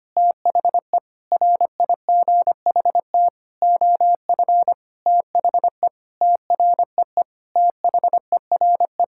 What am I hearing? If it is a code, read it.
Morse code, heard as a keyed tone, about 25 wpm, 700 Hz. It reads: THE RIGHT OF THE TREE THERE
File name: practice / 25wpm / Sentences with 9-Word Phrases - 25wpm